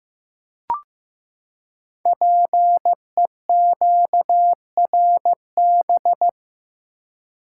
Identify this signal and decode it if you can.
Morse code, heard as a keyed tone, about 15 wpm, 700 Hz. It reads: PEQRB
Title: Code Groups - 5 Letters - 15wpm